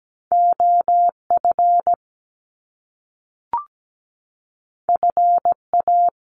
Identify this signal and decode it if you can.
Morse code, heard as a keyed tone, about 17 wpm, 700 Hz. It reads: OF FA
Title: Top Two Letter Combinations in Words 17wpm